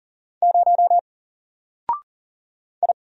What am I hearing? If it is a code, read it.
Morse code, heard as a keyed tone, about 40 wpm, 700 Hz. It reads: 0 I
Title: Single Letter-Number Alt 40wpm